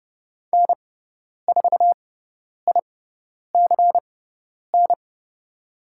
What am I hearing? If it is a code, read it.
Morse code, heard as a keyed tone, about 30 wpm, spaced as for 10 wpm, 700 Hz. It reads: N4ICN
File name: Lesson 16g - Letter c - Call Signs 30wpm